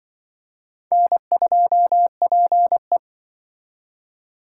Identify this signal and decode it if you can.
Morse code, heard as a keyed tone, about 24 wpm, 700 Hz. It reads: N2PE
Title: Speed Racing - USA 1x2 Call Signs 20wpm